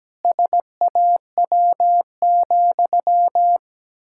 Morse code, heard as a keyed tone, about 17 wpm, 700 Hz. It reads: SAW,
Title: Pangram Challenge v2 - 17wpm